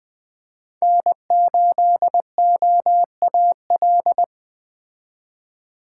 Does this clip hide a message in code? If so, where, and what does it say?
Yes: Morse code from 0.8 to 4.2 s, heard as a keyed tone, about 20 wpm, 700 Hz. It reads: N8OAL